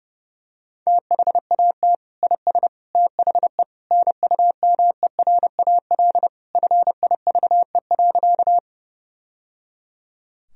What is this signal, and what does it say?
Morse code, heard as a keyed tone, about 32 wpm, 700 Hz. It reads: THAT IS THE NUMERAL FIVE.
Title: Sentences from Top 1000 Words - Review Part 2 - 30wpm